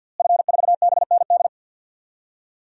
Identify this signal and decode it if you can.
Morse code, heard as a keyed tone, about 49 wpm, 700 Hz. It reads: F4BND
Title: DXCC-100 Call Signs - 5 Characters - Repeated 3x - 50wpm